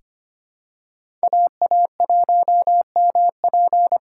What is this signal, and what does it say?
Morse code, heard as a keyed tone, about 25 wpm, 700 Hz. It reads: AA1MP
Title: USA Call Signs - 5 Characters - 4x Character Spacing - 25wpm